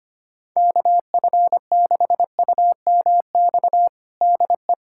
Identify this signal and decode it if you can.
Morse code, heard as a keyed tone, about 25 wpm, 700 Hz. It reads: KF6UMX DE
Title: USA Callsign DE Callsign - 6 Characters - 25wpm